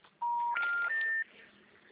Special information tone.